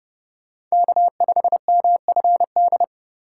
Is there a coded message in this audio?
Yes: Morse code from 0.7 to 2.8 s, heard as a keyed tone, about 38 wpm, 700 Hz. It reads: K5MFD